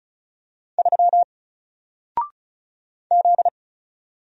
Morse code, heard as a keyed tone, about 35 wpm, 700 Hz. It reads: 3 Z
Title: Single Letter-Number Alt 35wpm